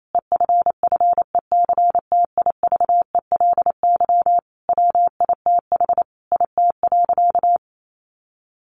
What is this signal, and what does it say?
Morse code, heard as a keyed tone, about 28 wpm, 700 Hz. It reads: EFFECTIVELY WITH IT.
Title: Sentences from Top 1400 Words 28wpm